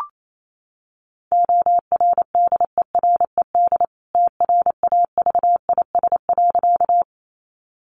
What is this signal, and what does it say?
Morse code, heard as a keyed tone, about 28 wpm, 700 Hz. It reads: ORDERED TRAVIS.
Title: A Sound Of Thunder 28wpm